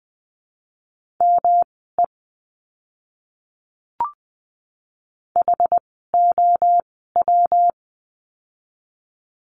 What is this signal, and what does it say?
Morse code, heard as a keyed tone, about 20 wpm, spaced as for 15 wpm, 700 Hz. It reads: ME HOW
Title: 244 Lesson 39f - Letter z - Words 20wpm